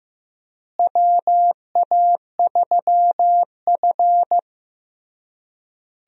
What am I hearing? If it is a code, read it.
Morse code, heard as a keyed tone, about 15 wpm, 700 Hz. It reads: WA3F
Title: USA Call Signs - 4 Characters - Repeated 3x - 15wpm